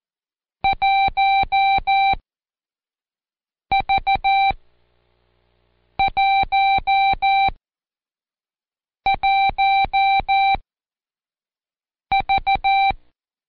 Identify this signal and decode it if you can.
Morse code, heard as a keyed tone, about 14 wpm, spaced as for 5 wpm, 780 Hz. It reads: 1V11V
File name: Code Quick / AudioDownld1 / Seg70-Day8-Intro-1